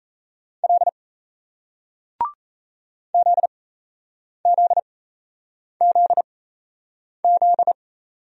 Morse code, heard as a keyed tone, about 38 wpm, spaced as for 8 wpm, 700 Hz. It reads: L ZZZZ